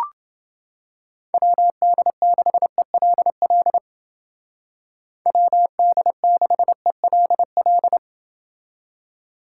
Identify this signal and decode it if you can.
Morse code, heard as a keyed tone, about 29 wpm, 700 Hz. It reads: WD6ELL WD6ELL